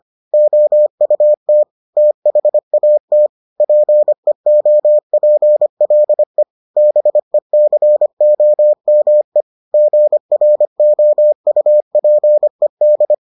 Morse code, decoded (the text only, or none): OUT THAT PEOPLE BECOME GROUPED